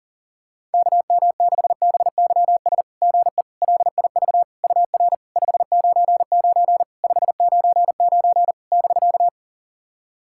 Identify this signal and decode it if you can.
Morse code, heard as a keyed tone, about 44 wpm, 690 Hz. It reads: KM6BYS GE LIV UR 599 599 <BK>
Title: SOTA QSO Practice 40wpm